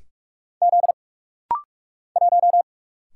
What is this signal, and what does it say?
Morse code, heard as a keyed tone, about 44 wpm, 700 Hz. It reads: Z 1
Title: Single Letter-Number - Rapid-Fire 45wpm